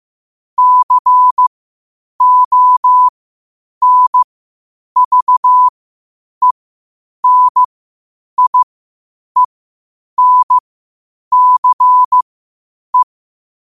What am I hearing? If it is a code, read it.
Morse code, heard as a keyed tone, about 15 wpm, spaced as for 8 wpm, 1000 Hz. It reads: CONVENIENCE
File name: qotdfiles / 8wpm